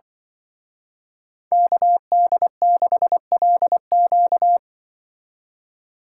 Morse code, heard as a keyed tone, about 24 wpm, 700 Hz. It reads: KD6LQ
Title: Speed Racing - USA 2x2 Call Signs 20wpm